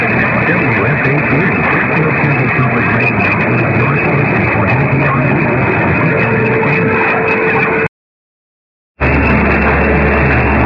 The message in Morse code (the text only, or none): none